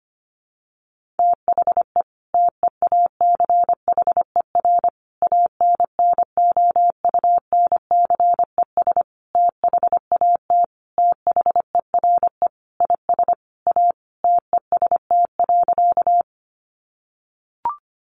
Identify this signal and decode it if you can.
Morse code, heard as a keyed tone, about 25 wpm, 700 Hz. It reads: THE TEACHER ANNOUNCES THAT THERE IS A TEST.
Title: Additional Easy Phrases 25wpm